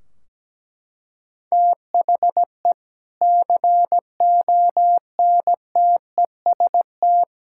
Morse code, heard as a keyed tone, about 17 wpm, 700 Hz. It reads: THE CONTEST
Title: Sentences from Top 1600 Words 17wpm